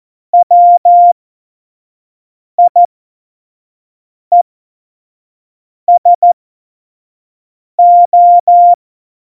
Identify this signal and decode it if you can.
Morse code, heard as a keyed tone, about 14 wpm, spaced as for 5 wpm, 700 Hz. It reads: WIESO
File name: Morsekurs / Witze / Witz 18 Juli 17, Tempo 25 F70